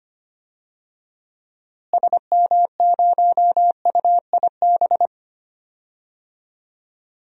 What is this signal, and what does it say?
Morse code, heard as a keyed tone, about 25 wpm, 700 Hz. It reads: SM0UIB
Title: DXCC-100 Call Signs - 6 Characters - Repeated 3x - 25wpm